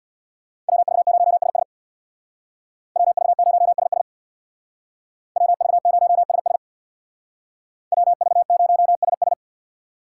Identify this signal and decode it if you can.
Morse code, heard as a keyed tone, about 62 wpm, 700 Hz. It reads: WV0SS WV0SS WV0SS WV0SS
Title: Speed Racing - USA 2x2 Call Signs 50wpm